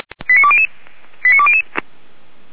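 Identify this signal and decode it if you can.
Five-tone selective call (ZVEI), heard as quick runs of five tones, tones of about 80 ms. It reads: 89200 89200
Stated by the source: ZVEI five-tone selective call 89494 received by SDR, archived by Feuerwehr Orlamünde.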